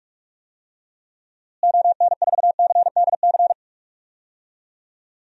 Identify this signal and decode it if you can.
Morse code, heard as a keyed tone, about 46 wpm, 700 Hz. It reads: ON4CDC
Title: DXCC-100 Call Signs - 6 Characters - Repeated 3x - 45wpm